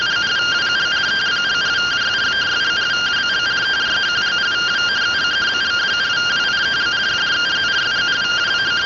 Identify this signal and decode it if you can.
Two-tone teleprinter signal, about 170 Hz shift.